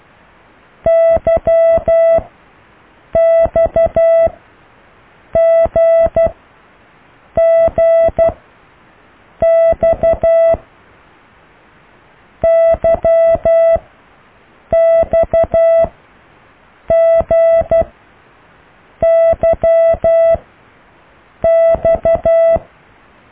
Morse code, heard as a keyed tone, about 12 wpm, 660 Hz. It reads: Y X G G X Y X G Y X